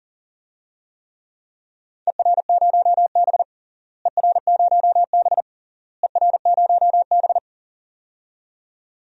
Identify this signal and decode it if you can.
Morse code, heard as a keyed tone, about 37 wpm, 700 Hz. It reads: ER0B ER0B ER0B
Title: DXCC-100 Call Signs - 4 Characters - Repeated 3x - 40wpm